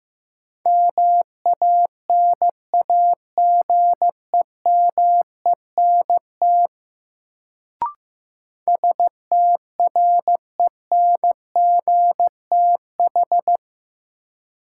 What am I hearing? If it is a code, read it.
Morse code, heard as a keyed tone, about 15 wpm, 700 Hz. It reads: MANAGEMENT STRENGTH
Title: Sentences from Top 1000 Words - Review Part 4 - New Word Drill - 15wpm